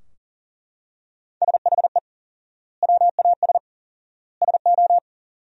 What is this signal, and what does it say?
Morse code, heard as a keyed tone, about 37 wpm, 700 Hz. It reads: SHE WAS SO
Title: 4x Word-Spacing - Sentences from Top 400 Words - 40wpm